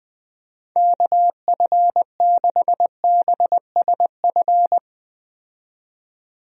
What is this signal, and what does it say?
Morse code, heard as a keyed tone, about 20 wpm, 700 Hz. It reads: KF6BSF
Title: USA Call Signs - 6 Characters - 20wpm